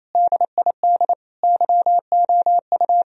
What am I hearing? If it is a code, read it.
Morse code, heard as a keyed tone, about 28 wpm, 700 Hz. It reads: DID YOU
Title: Sentences from Top 1000 Words - Review Part 1 - 28wpm